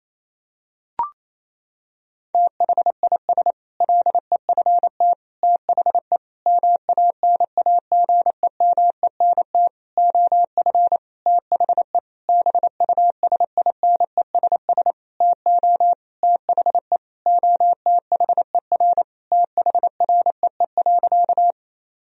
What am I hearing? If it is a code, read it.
Morse code, heard as a keyed tone, about 28 wpm, 700 Hz. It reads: THIS LEFT THE MANAGEMENT OF THE BUSINESS TO THE OTHER THREE.
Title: Sentences from Top 1000 Words - Review Part 4 - 28wpm